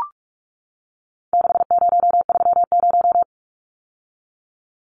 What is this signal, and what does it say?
Morse code, heard as a keyed tone, about 44 wpm, 700 Hz. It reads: 6030
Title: Numbers - 4-Digit 45wpm